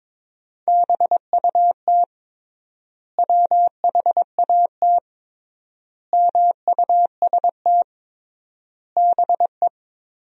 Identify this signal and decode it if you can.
Morse code, heard as a keyed tone, about 22 wpm, 700 Hz. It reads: BUT WHAT MUST BE
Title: 3x Word-Spacing - Sentences from Top 500 Words - 22wpm